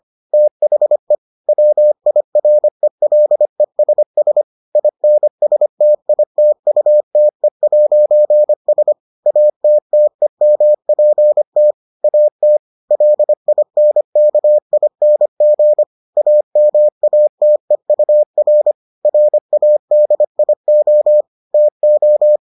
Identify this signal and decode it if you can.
Morse code, heard as a keyed tone, about 25 wpm, 600 Hz. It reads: THE WIRELESS INSTITUTE'S ATTEMPT AT LINKING AMATEUR RADIO TO